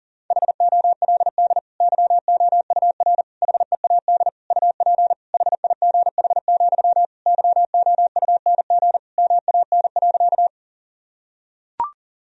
Morse code, heard as a keyed tone, about 40 wpm, 700 Hz. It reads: HOLD YOUR HEAD UP HIGH, YOUNG MAN.
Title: Sentences from Top 1000 Words - Review Part 2 - 40wpm